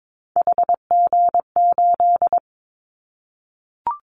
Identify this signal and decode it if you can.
Morse code, heard as a keyed tone, about 22 wpm, 700 Hz. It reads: HG8
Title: Callsign Prefixes for Top 50 DXCC - 22wpm